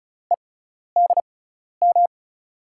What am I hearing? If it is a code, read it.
Morse code, heard as a keyed tone, about 36 wpm, spaced as for 12 wpm, 700 Hz. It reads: EDM
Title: USA Call Signs - 5 Characters - 6x Character Spacing - 35wpm